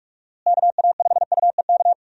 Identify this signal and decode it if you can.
Morse code, heard as a keyed tone, about 45 wpm, 700 Hz. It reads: KA5UEK
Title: USA Call Signs - 6 Characters - Repeated 3x - 45wpm